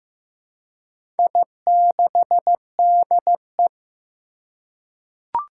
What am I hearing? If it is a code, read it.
Morse code, heard as a keyed tone, about 15 wpm, 700 Hz. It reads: I6DE
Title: Four Challenging Letter-Number Combinations 15wpm